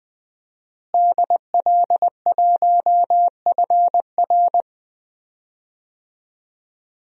Morse code, heard as a keyed tone, about 20 wpm, 700 Hz. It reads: DL1FR
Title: DXCC-100 Call Signs - 5 Characters - Repeated 2x - 20wpm